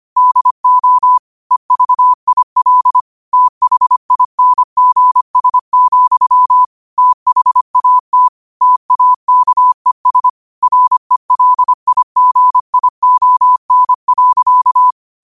Morse code, heard as a keyed tone, about 25 wpm, 1000 Hz. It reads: DO EVIL THINGS, THAT TAKES RELIGION.
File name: qotdfiles / 25wpm